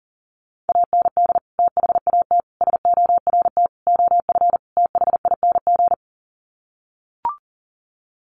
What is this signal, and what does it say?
Morse code, heard as a keyed tone, about 40 wpm, 700 Hz. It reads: AND THAT SORT OF THING
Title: Sets of 5 Words Encore2 40wpm